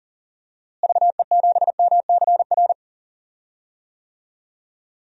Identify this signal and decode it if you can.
Morse code, heard as a keyed tone, about 40 wpm, 700 Hz. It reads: VE7MCR
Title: DXCC-100 Call Signs - 6 Characters - Repeated 3x - 40wpm